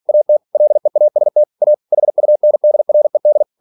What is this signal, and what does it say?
Morse code, heard as a keyed tone, about 46 wpm, 600 Hz. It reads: AT LEAST A HUNDRED
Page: Never 47wpm